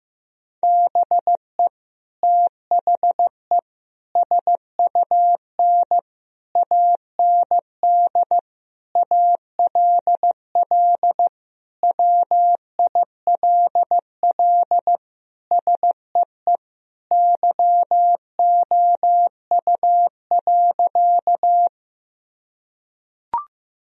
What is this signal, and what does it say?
Morse code, heard as a keyed tone, about 15 wpm, 700 Hz. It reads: BE THE SUN AND ALL WILL SEE YOU.
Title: Sentences from Top 1000 Words - Review Part 1 - 15wpm